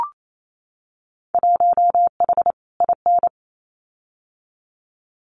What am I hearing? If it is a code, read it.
Morse code, heard as a keyed tone, about 28 wpm, 700 Hz. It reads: 1H IN
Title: Winter Field Day Exchanges 28wpm